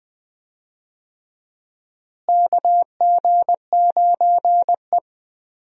Morse code, heard as a keyed tone, about 20 wpm, 700 Hz. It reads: KG9E
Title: USA Call Signs - 4 Characters - Repeated 2x - 20wpm